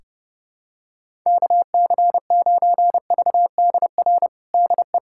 Morse code, heard as a keyed tone, about 30 wpm, 700 Hz. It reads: KC9VDR DE
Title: USA Callsign DE Callsign 30wpm